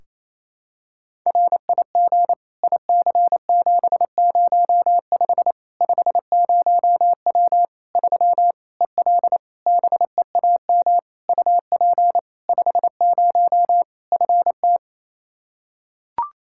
Morse code, heard as a keyed tone, about 28 wpm, 700 Hz. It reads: RIG IC705 50W 3 EL BEAM UP 50 FT
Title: Practice QSO - Exchange 2 - Info - 28wpm